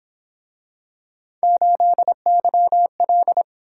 Morse code, heard as a keyed tone, about 27 wpm, 700 Hz. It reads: 8YL
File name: Speed-Racing - Three Letter-Number Combinations - 22wpm